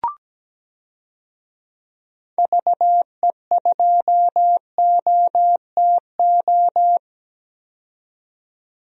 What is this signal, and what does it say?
Morse code, heard as a keyed tone, about 17 wpm, 700 Hz. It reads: VE2OTO
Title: DXCC-100 Call Signs - 6 Characters - 17wpm